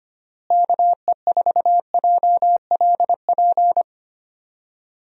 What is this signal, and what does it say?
Morse code, heard as a keyed tone, about 25 wpm, 700 Hz. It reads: KE4JLP
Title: USA Call Signs - 6 Characters - Repeated 3x - 25wpm